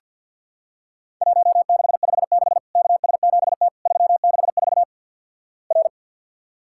Morse code, heard as a keyed tone, about 50 wpm, 700 Hz. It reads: 1656 KS7T 364 R
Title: Worked All Europe DX Contest - QTC Exchange with Repeats - 50wpm